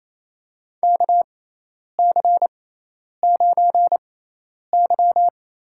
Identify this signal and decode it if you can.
Morse code, heard as a keyed tone, about 28 wpm, spaced as for 10 wpm, 700 Hz. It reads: KC9Y
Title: USA Call Signs - 6 Characters - 6x Character Spacing - 28wpm